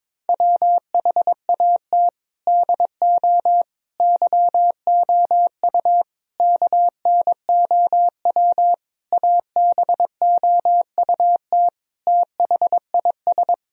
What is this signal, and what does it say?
Morse code, heard as a keyed tone, about 22 wpm, 700 Hz. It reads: WHAT DO YOU KNOW ABOUT THIS
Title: Sentences from Top 1000 Words - Review Part 2 - 22wpm